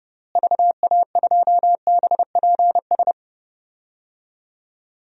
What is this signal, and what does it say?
Morse code, heard as a keyed tone, about 30 wpm, 700 Hz. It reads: VA2BPS
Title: DXCC-100 Call Signs - 6 Characters - Repeated 2x - 30wpm